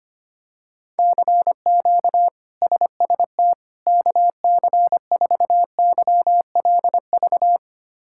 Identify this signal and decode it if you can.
Morse code, heard as a keyed tone, about 25 wpm, 700 Hz. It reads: CQ SST KC4YLV
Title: K1USN SST QSO Practice - 25wpm